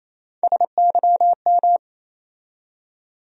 Morse code, heard as a keyed tone, about 28 wpm, 700 Hz. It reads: SYM
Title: USA Call Sign Postfixes - 3 Letters 28wpm